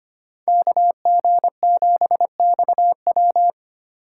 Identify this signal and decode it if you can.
Morse code, heard as a keyed tone, about 25 wpm, 700 Hz. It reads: KG7XW
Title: USA Call Signs - 5 Characters - 2x Character Spacing - 25wpm